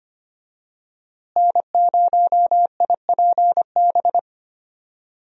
Morse code, heard as a keyed tone, about 25 wpm, 700 Hz. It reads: N0IPB
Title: USA Call Signs - 5 Characters - Repeated 2x - 25wpm